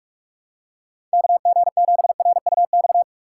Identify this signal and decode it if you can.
Morse code, heard as a keyed tone, about 47 wpm, 700 Hz. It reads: KG7RUX
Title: USA Call Signs - 6 Characters - 3x Character Spacing - 45wpm